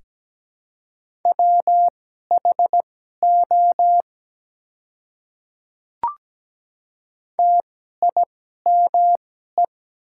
Morse code, heard as a keyed tone, about 17 wpm, spaced as for 12 wpm, 700 Hz. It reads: WHO TIME
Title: 2x Character Spacing - Top 200 Words - 17wpm